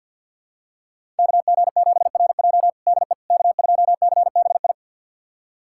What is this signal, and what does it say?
Morse code, heard as a keyed tone, about 52 wpm, 700 Hz. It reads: KG7RJ DE K2CBI
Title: USA Callsign DE Callsign - 5 Characters - 50wpm